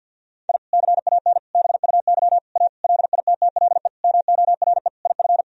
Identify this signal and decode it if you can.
Morse code, heard as a keyed tone, about 52 wpm, 700 Hz. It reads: I CAN BUY A LITTLE MORE IF
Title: Sentences from Top 1000 Words - Review Part 1 - 50wpm